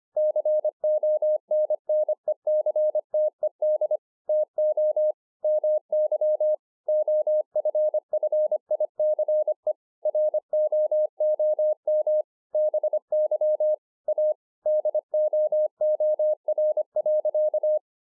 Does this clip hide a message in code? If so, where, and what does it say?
Yes: Morse code from 0.2 to 17.8 s, heard as a keyed tone, about 25 wpm, 610 Hz. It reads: CONNECTED TO MY OFFICE ROOM BY A DOOR.